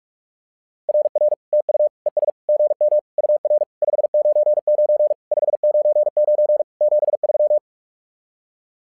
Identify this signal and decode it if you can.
Morse code, heard as a keyed tone, about 44 wpm, 600 Hz. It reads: RR TU ES GM UR 599 599 73
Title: SOTA QSO Practice 45wpm